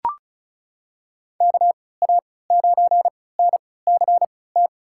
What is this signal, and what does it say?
Morse code, heard as a keyed tone, about 36 wpm, spaced as for 20 wpm, 700 Hz. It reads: KA9NCT
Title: USA Call Signs - 6 Characters - 3x Character Spacing - 35wpm